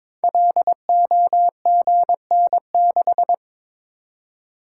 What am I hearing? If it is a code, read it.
Morse code, heard as a keyed tone, about 22 wpm, 700 Hz. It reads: LOGN6
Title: Code Groups - 5 Letter:Numbers - 22wpm